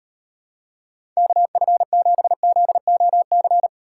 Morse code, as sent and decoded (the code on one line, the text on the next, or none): -.- ..-. --... --.. --- -.-.
KF7ZOC